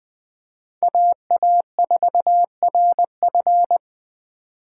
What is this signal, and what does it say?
Morse code, heard as a keyed tone, about 20 wpm, 700 Hz. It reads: AA4RF
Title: 2023 POTA Activator Call Signs - 20wpm